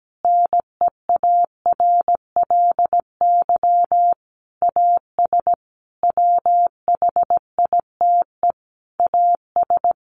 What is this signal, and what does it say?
Morse code, heard as a keyed tone, about 17 wpm, 700 Hz. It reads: NEARLY AS WHITE AS
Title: Sentences from Top 1500 Words 17wpm